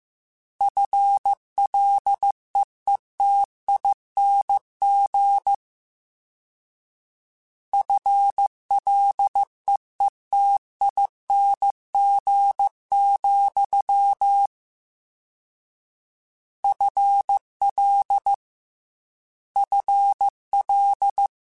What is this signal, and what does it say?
Morse code, heard as a keyed tone, about 15 wpm, 800 Hz. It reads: FLEETING FLEETING, FL FL
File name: Combinations / Combinations-mp3s / Combinations@13